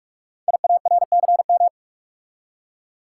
Morse code, heard as a keyed tone, about 44 wpm, 700 Hz. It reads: IARCM
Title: Code Groups - 5 Letters - 45wpm